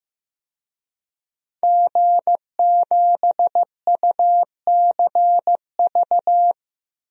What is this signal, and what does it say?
Morse code, heard as a keyed tone, about 16 wpm, 700 Hz. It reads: G7UCV
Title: DXCC-100 Call Signs - 5 Characters - Repeated 2x - 15wpm